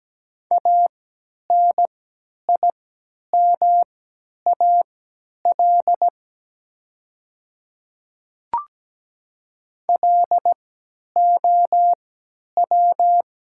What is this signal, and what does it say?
Morse code, heard as a keyed tone, about 17 wpm, spaced as for 10 wpm, 700 Hz. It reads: ANIMAL LOW
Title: 3x Character Spacing - Top 200 Words - 17wpm